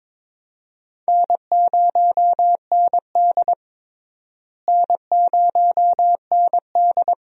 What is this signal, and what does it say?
Morse code, heard as a keyed tone, about 21 wpm, 700 Hz. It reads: N0ND N0ND
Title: USA Call Signs - 4 Characters - Repeated 3x - 22wpm